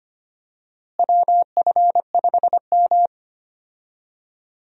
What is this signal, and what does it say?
Morse code, heard as a keyed tone, about 25 wpm, 700 Hz. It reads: WF5M
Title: USA Call Signs - 4 Characters - Repeated 3x - 25wpm